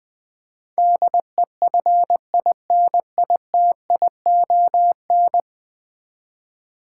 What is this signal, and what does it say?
Morse code, heard as a keyed tone, about 20 wpm, 700 Hz. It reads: DEFINITION
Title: 10 Letter Words - First 3 Letters Given - 20wpm